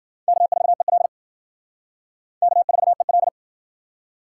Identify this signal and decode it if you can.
Morse code, heard as a keyed tone, about 60 wpm, 700 Hz. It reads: K4EL K4EL